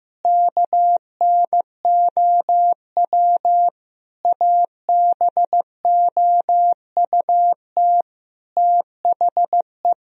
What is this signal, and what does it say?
Morse code, heard as a keyed tone, about 15 wpm, 700 Hz. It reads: KNOW ABOUT THE
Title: Sentences from Top 1000 Words - Review Part 4 - 15wpm